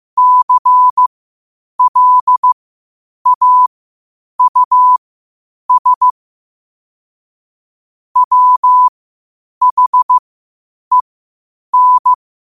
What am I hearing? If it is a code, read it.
Morse code, heard as a keyed tone, about 15 wpm, spaced as for 8 wpm, 1000 Hz. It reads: CLAUS WHEN
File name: qotdfiles / 8wpm